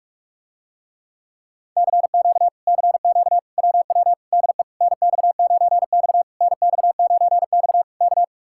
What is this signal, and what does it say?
Morse code, heard as a keyed tone, about 49 wpm, 700 Hz. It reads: CQ CQ WW DE NX9X NX9X K